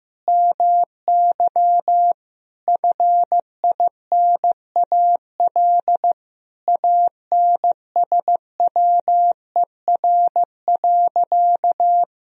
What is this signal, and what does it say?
Morse code, heard as a keyed tone, about 15 wpm, 700 Hz. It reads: MY FINAL ANSWER.
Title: Sentences from Top 1000 Words - Review Part 2 - 15wpm